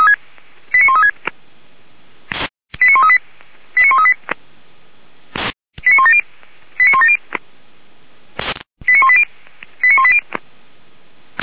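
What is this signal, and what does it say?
Five-tone selective call (ZVEI), heard as quick runs of five tones, tones of about 70 ms. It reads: 89136 89138 89138 89170 89170 89180 89180
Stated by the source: ZVEI five-tone selective call received by SDR.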